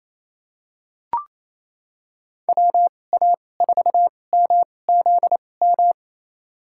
Morse code, heard as a keyed tone, about 28 wpm, spaced as for 20 wpm, 700 Hz. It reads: WA4MZM
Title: USA Call Signs - 6 Characters - 2x Character Spacing - 28wpm